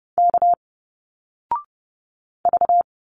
Morse code, heard as a keyed tone, about 30 wpm, 700 Hz. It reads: K V
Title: Single Letters 30wpm